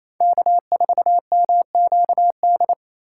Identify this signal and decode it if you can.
Morse code, heard as a keyed tone, about 33 wpm, 700 Hz. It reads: K4MQD
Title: Speed Racing - USA 1x3 Call Signs 28wpm